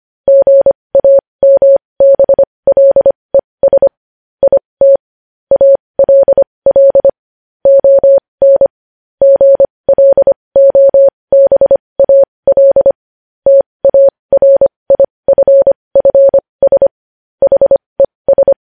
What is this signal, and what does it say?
Morse code, heard as a keyed tone, about 25 wpm, spaced as for 20 wpm, 550 Hz. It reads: GAMBLES IT ALL ON GLOBAL TARIFFS HES
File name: podcast / News headlines-1x-25x20